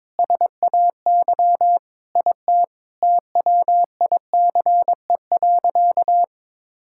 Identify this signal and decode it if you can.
Morse code, heard as a keyed tone, about 22 wpm, 700 Hz. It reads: SAY IT TWICE.
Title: Sentences from Top 1600 Words 22wpm